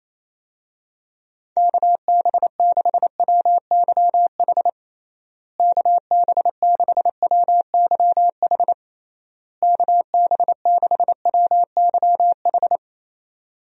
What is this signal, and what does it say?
Morse code, heard as a keyed tone, about 28 wpm, 700 Hz. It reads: KB6WYH KB6WYH KB6WYH
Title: USA Call Signs - 6 Characters - Repeated 3x - 28wpm